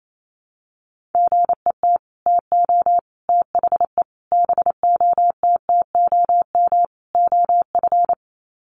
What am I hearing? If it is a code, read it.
Morse code, heard as a keyed tone, about 28 wpm, 700 Hz. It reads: GET TO THE BOTTOM OF
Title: Sets of 5 Words Encore 28wpm